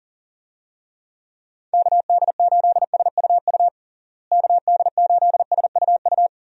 Morse code, heard as a keyed tone, about 40 wpm, 700 Hz. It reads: KD8SUU KD8SUU
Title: USA Call Signs - 6 Characters - Repeated 2x - 40wpm